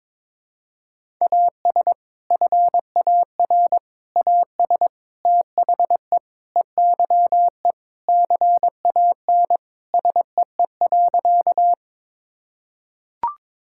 Morse code, heard as a keyed tone, about 22 wpm, 700 Hz. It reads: AS FAR AS THE EYE CAN SEE.
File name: practice / 22wpm / Sentences with 4-Word Phrases - 22wpm